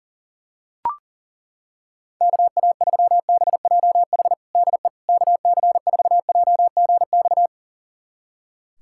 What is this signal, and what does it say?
Morse code, heard as a keyed tone, about 40 wpm, 700 Hz. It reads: KA3BJH DE KC4JGX
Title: USA Callsign DE Callsign - 6 Characters - 40wpm